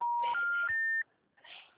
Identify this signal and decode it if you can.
Special information tone.